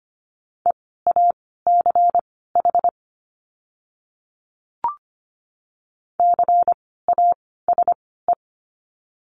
Morse code, heard as a keyed tone, about 25 wpm, spaced as for 16 wpm, 700 Hz. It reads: EACH CASE
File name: course / 25:10wpm / 094 Lesson 16f - Letter c - Words 25wpm